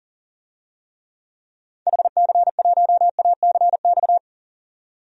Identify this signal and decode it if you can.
Morse code, heard as a keyed tone, about 40 wpm, 700 Hz. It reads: HC1ACX
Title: DXCC-100 Call Signs - 6 Characters - 40wpm